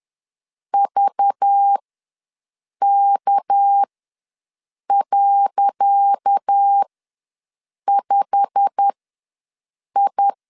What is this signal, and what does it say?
Morse code, heard as a keyed tone, about 10 wpm, spaced as for 6 wpm, 800 Hz. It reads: VK.5I
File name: LEZIONI / 10 SIGNS / 10D COMA ws 10wpm 4314